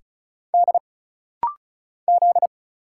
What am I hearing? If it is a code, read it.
Morse code, heard as a keyed tone, about 36 wpm, 700 Hz. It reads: D Z
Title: Single Letter-Number Alt - Rapid-Fire 35wpm